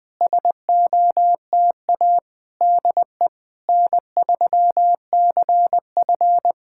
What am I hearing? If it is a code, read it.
Morse code, heard as a keyed tone, about 20 wpm, 700 Hz. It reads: SOTA DE N3CF
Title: SOTA QSO Practice 20wpm